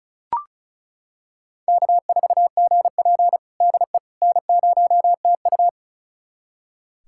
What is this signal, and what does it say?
Morse code, heard as a keyed tone, about 34 wpm, 700 Hz. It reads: K4GP DE N0TU
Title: USA Callsign DE Callsign - 4 Characters - 35wpm